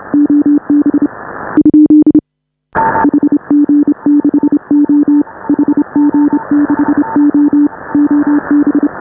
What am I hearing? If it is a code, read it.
Morse code, heard as a keyed tone, about 29 wpm, 290 Hz. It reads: OB ? HG6O HG6O OB
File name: AUDIO2019 / HA5BA CW